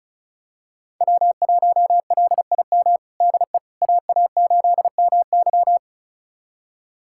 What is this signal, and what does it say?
Morse code, heard as a keyed tone, about 36 wpm, 700 Hz. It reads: W1LIM DE AA8MY
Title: USA Callsign DE Callsign - 5 Characters - 35wpm